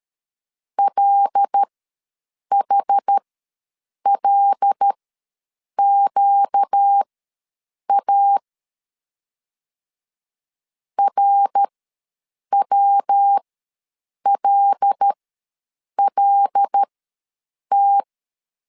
Morse code, heard as a keyed tone, about 13 wpm, spaced as for 7 wpm, 800 Hz. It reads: LHLQA RWLLT
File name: LEZIONI / 5 LBVC / 5 L ws 12wpm 2958